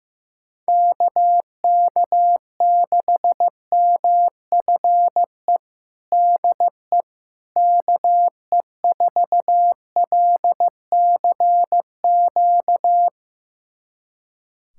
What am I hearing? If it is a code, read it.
Morse code, heard as a keyed tone, about 15 wpm, 700 Hz. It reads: KK6MFE DE KE4LCQ
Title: USA Callsign DE Callsign - 6 Characters - 15wpm